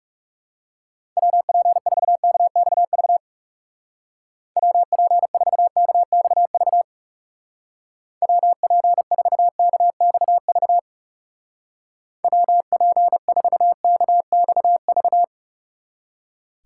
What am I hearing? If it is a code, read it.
Morse code, heard as a keyed tone, about 37 wpm, 700 Hz. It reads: WP4KXV WP4KXV WP4KXV WP4KXV
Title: Speed Racing - USA 2x3 Call Signs 30wpm